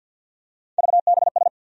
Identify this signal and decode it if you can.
Morse code, heard as a keyed tone, about 47 wpm, 700 Hz. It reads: VBS
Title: USA Call Sign Postfixes - 3 Letters 50wpm